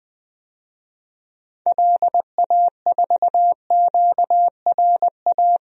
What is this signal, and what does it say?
Morse code, heard as a keyed tone, about 20 wpm, 700 Hz. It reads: LA4QRA
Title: DXCC-100 Call Signs - 6 Characters - Repeated 3x - 20wpm